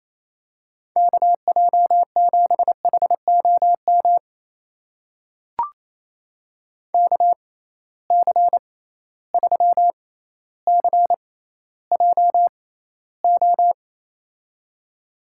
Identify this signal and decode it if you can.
Morse code, heard as a keyed tone, about 28 wpm, spaced as for 14 wpm, 700 Hz. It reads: KJ7HOM KC3CJO
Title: USA Call Signs - 6 Characters - 6x Character Spacing - 28wpm